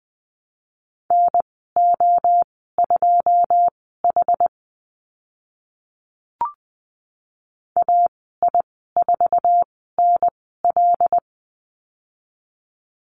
Morse code, heard as a keyed tone, about 20 wpm, spaced as for 14 wpm, 700 Hz. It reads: NO2H AI4NL
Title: 082 Lesson 14g - Number 2 - Call Signs 20wpm